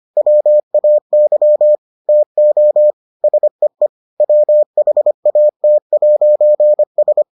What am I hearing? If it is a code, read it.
Morse code, heard as a keyed tone, about 25 wpm, 600 Hz. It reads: WAY TO SEE WHAT'S